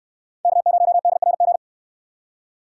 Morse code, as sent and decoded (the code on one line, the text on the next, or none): -.- ----- -.. .-. --.
K0DRG